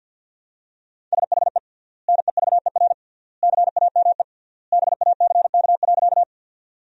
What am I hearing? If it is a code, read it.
Morse code, heard as a keyed tone, about 47 wpm, 700 Hz. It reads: SHE NEVER CAME BACK.
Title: 3x Word-Spacing - Sentences from Top 300 Words - 50wpm